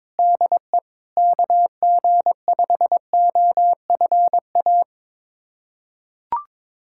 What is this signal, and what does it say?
Morse code, heard as a keyed tone, about 22 wpm, 700 Hz. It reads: DE KG5OFA
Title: USA Callsign DE Callsign - 6 Characters - 22wpm